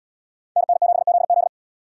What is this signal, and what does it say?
Morse code, heard as a keyed tone, about 73 wpm, 700 Hz. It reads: NN7ZZ